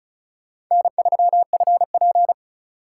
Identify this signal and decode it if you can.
Morse code, heard as a keyed tone, about 43 wpm, 700 Hz. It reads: N3FP
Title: Speed Racing - USA 1x2 Call Signs 35wpm